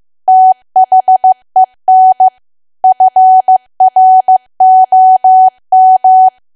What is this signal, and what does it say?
Morse code, heard as a keyed tone, about 15 wpm, 750 Hz. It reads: THEN FROM
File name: cwa / W1AW-Level2 / 15WPM